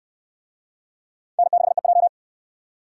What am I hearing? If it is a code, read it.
Morse code, heard as a keyed tone, about 68 wpm, 700 Hz. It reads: N6EJ